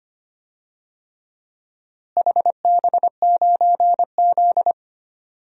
Morse code, heard as a keyed tone, about 25 wpm, 700 Hz. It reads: HB9Z
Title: DXCC-100 Call Signs - 4 Characters - Repeated 3x - 25wpm